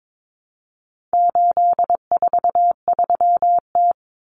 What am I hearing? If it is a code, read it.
Morse code, heard as a keyed tone, about 22 wpm, 700 Hz. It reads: 843T
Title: Common Cut Numbers - 4-Digit 22wpm